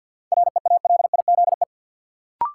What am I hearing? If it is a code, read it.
Morse code, heard as a keyed tone, about 52 wpm, 700 Hz. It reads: REALIZE
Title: Sentences from Top 1000 Words - New Word Drill - 50wpm